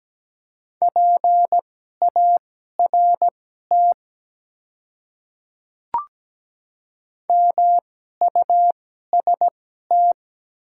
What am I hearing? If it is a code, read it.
Morse code, heard as a keyed tone, about 17 wpm, spaced as for 12 wpm, 700 Hz. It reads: PART MUST
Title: 2x Character Spacing - Top 200 Words - 17wpm